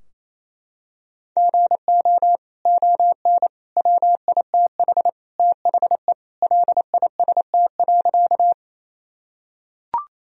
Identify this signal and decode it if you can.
Morse code, heard as a keyed tone, about 27 wpm, 700 Hz. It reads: GO ON WITH THE LIST.
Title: Sentences from Top 1000 Words - Review Part 2 - 28wpm